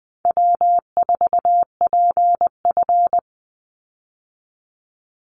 Morse code, heard as a keyed tone, about 20 wpm, 700 Hz. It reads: W4PF
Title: Easy Call Signs - 20wpm